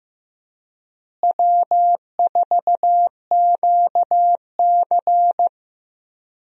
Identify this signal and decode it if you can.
Morse code, heard as a keyed tone, about 15 wpm, 700 Hz. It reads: W4QC
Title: USA Call Signs - 4 Characters - Repeated 2x - 15wpm